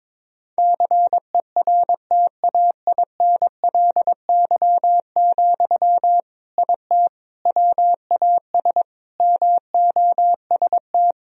Morse code, heard as a keyed tone, about 22 wpm, 700 Hz. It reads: CERTAINLY, IT WAS MOST
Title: Sentences from Top 1000 Words - Review Part 1 - 22wpm